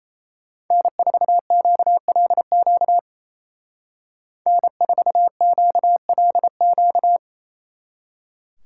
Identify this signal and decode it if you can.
Morse code, heard as a keyed tone, about 33 wpm, 700 Hz. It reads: N4QLQ N4QLQ
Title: Speed Racing - USA 1x3 Call Signs 28wpm